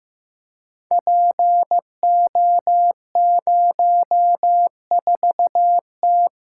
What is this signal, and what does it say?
Morse code, heard as a keyed tone, about 15 wpm, 700 Hz. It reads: PO04T
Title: Code Groups - 5 Letter:Numbers - 15wpm